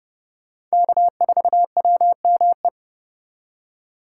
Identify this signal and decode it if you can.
Morse code, heard as a keyed tone, about 37 wpm, 700 Hz. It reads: K4WME
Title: Speed Racing - USA Call Signs 30wpm